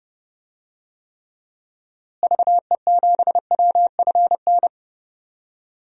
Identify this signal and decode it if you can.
Morse code, heard as a keyed tone, about 31 wpm, 700 Hz. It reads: VE7WFN
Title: DXCC-100 Call Signs - 6 Characters - Repeated 2x - 30wpm